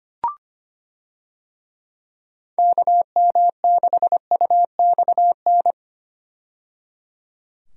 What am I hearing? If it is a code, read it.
Morse code, heard as a keyed tone, about 25 wpm, 700 Hz. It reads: KM6UXN
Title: USA Call Signs - 6 Characters - 25wpm